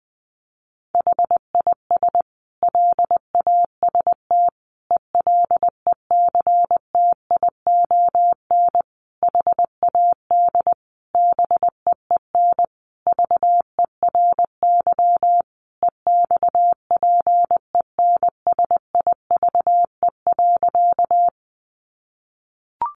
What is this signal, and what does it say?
Morse code, heard as a keyed tone, about 20 wpm, 700 Hz. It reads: HIS LAST ELECTION HAD BEEN VERY EXPENSIVE.
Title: Sentences from Top 1500 Words 20wpm